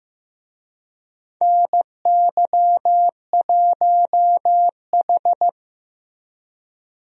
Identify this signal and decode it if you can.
Morse code, heard as a keyed tone, about 15 wpm, 700 Hz. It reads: NY1H
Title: USA Call Signs - 4 Characters - Repeated 2x - 15wpm